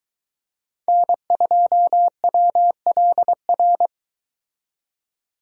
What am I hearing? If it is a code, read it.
Morse code, heard as a keyed tone, about 22 wpm, 700 Hz. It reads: N2WLR